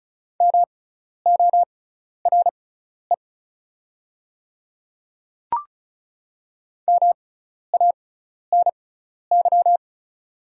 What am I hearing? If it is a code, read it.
Morse code, heard as a keyed tone, about 36 wpm, spaced as for 12 wpm, 700 Hz. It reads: MORE MANY
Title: 6x Character Spacing - Top 100 Words - 35wpm